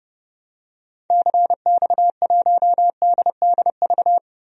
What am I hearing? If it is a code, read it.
Morse code, heard as a keyed tone, about 32 wpm, 700 Hz. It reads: CX1DDV
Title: DXCC-100 Call Signs - 6 Characters - 30wpm